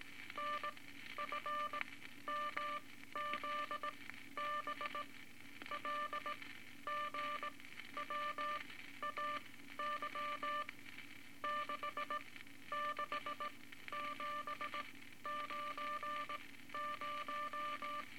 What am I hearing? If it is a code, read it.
Morse code, heard as a keyed tone, about 18 wpm, spaced as for 12 wpm, 1180 Hz, with noise. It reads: NFMZB LGWAY 66790
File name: Morse Code Records / 18 A LCM 164 Ten Words Per Minute